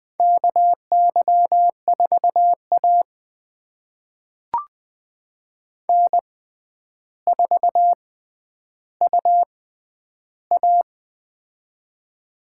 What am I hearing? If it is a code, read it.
Morse code, heard as a keyed tone, about 20 wpm, spaced as for 10 wpm, 700 Hz. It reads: KY4A N4UA
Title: USA Call Signs - 4 Characters - 6x Character Spacing - 20wpm